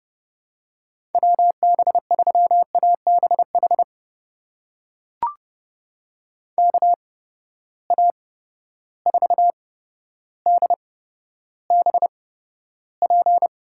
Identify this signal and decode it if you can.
Morse code, heard as a keyed tone, about 32 wpm, spaced as for 13 wpm, 700 Hz. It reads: WB3ABH KA4DBP
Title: USA Call Signs - 6 Characters - 8x Character Spacing - 30wpm